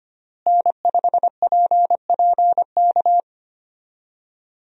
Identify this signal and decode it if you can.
Morse code, heard as a keyed tone, about 25 wpm, 700 Hz. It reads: N5PPK
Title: USA Call Signs - 5 Characters - Repeated 3x - 25wpm